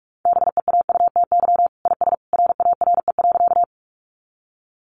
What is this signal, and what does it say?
Morse code, heard as a keyed tone, about 44 wpm, 700 Hz. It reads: BEAUTY IS RARE.
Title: Sentences from Top 1100 Words 45wpm